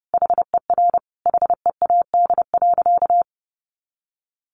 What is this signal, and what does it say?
Morse code, heard as a keyed tone, about 30 wpm, 700 Hz. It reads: HER HEAD.
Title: Sentences from Top 1400 Words 30wpm